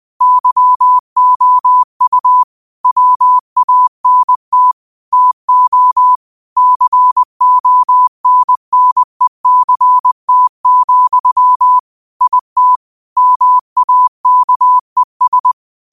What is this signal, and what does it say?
Morse code, heard as a keyed tone, about 20 wpm, 1000 Hz. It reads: YOU WANT TO CONNECT, IT MAKES